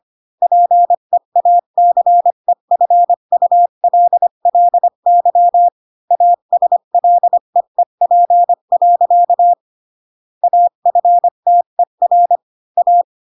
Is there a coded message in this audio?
Yes: Morse code from 0.4 to 13.0 s, heard as a keyed tone, about 25 wpm, 700 Hz. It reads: PEACEFULLY ASLEEP. AFTER A